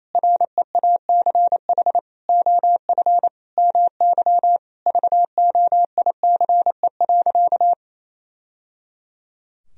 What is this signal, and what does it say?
Morse code, heard as a keyed tone, about 28 wpm, 700 Hz. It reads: REACH OF MY VOICE.
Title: Sentences from Top 1000 Words - Review Part 4 - 28wpm